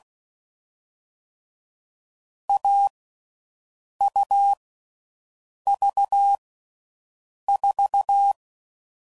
Morse code, heard as a keyed tone, about 16 wpm, 800 Hz. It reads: A U V 4
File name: Combinations / Difficult Combos / AUV AUV4